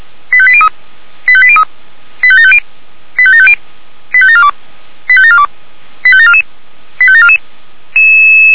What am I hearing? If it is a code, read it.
Five-tone selective call (ZVEI), heard as quick runs of five tones, tones of about 70 ms. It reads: 85803 85803 85750 85750 85742 85742 85744 85744
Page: ZVEI